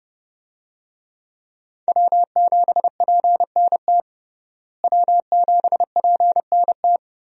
Morse code, heard as a keyed tone, about 30 wpm, 700 Hz. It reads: W7PNT W7PNT
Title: USA Call Signs - 5 Characters - Repeated 2x - 30wpm